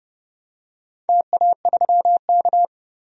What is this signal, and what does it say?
Morse code, heard as a keyed tone, about 30 wpm, 700 Hz. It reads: TA3K
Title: DXCC-100 Call Signs - 4 Characters - 30wpm